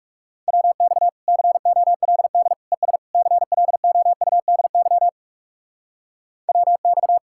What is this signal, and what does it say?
Morse code, heard as a keyed tone, about 40 wpm, 700 Hz. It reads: WX COLD ES CLOUDY WX